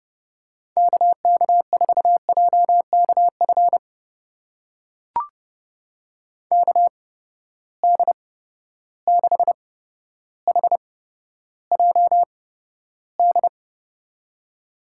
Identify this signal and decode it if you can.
Morse code, heard as a keyed tone, about 30 wpm, spaced as for 13 wpm, 700 Hz. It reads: KK4JKF KD6HJD
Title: USA Call Signs - 6 Characters - 8x Character Spacing - 30wpm